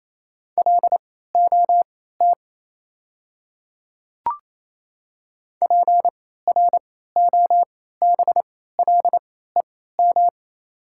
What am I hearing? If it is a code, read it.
Morse code, heard as a keyed tone, about 28 wpm, spaced as for 16 wpm, 700 Hz. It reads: LOT PROBLEM
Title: 3x Character Spacing - Top 500 Words - 28wpm